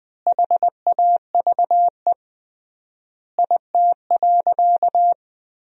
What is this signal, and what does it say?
Morse code, heard as a keyed tone, about 20 wpm, 700 Hz. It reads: HAVE IT.
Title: 3x Word-Spacing - Sentences from Top 100 Words - 20wpm